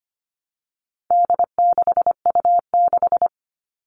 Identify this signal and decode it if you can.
Morse code, heard as a keyed tone, about 25 wpm, 700 Hz. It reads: D6U6
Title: All Cut Numbers - 4-Digit 25wpm